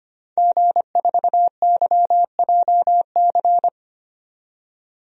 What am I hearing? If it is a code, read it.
Morse code, heard as a keyed tone, about 25 wpm, 700 Hz. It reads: G4YJC
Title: DXCC-100 Call Signs - 5 Characters - 25wpm